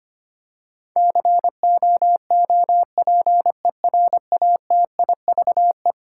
Morse code, heard as a keyed tone, about 25 wpm, 700 Hz. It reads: COOPERATIVE
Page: morse-code-ninja-practice-25wpm